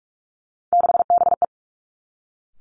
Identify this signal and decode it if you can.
Morse code, heard as a keyed tone, about 46 wpm, 700 Hz. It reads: 6BE
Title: Three Challenging Letter-Number Combinations 45wpm